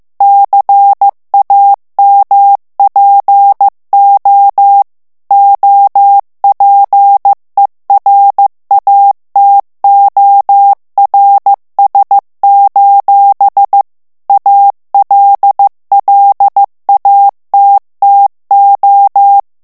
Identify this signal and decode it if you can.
Morse code, heard as a keyed tone, about 15 wpm, 800 Hz. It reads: CAMPO OPERATORS: ALLATTO
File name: morse mp3 / ITALIEN texte clair 2 vitesse 14 mots minute